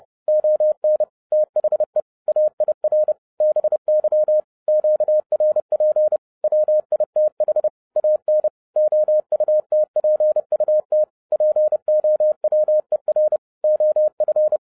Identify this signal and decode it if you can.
Morse code, heard as a keyed tone, about 30 wpm, 610 Hz. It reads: ON THE AIR BY QRP WITH AN OUTPUT POWER OF
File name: 2011-07-mp3 / n40 EXCITING QRP CW OPERATION 30wpm